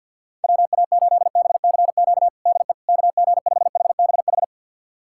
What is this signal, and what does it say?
Morse code, heard as a keyed tone, about 50 wpm, 690 Hz. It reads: WA8BCQ DE KG5HBH
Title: USA Callsign DE Callsign - 6 Characters - 50wpm